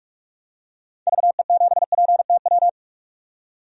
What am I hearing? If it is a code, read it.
Morse code, heard as a keyed tone, about 44 wpm, 700 Hz. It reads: VE7PTW
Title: DXCC-100 Call Signs - 6 Characters - 45wpm